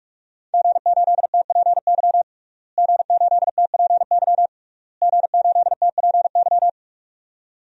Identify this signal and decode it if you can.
Morse code, heard as a keyed tone, about 46 wpm, 700 Hz. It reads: G8TPY G8TPY G8TPY
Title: DXCC-100 Call Signs - 5 Characters - Repeated 3x - 45wpm